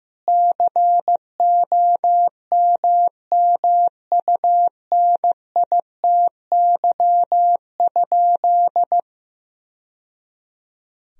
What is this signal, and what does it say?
Morse code, heard as a keyed tone, about 16 wpm, 700 Hz. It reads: COMMUNITY?
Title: Sentences from Top 1000 Words - Review Part 4 - 15wpm